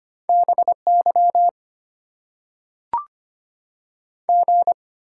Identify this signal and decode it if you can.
Morse code, heard as a keyed tone, about 25 wpm, 690 Hz. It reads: BY G
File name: Top 90 QSO Elements - 8x Character Spacing - 25wpm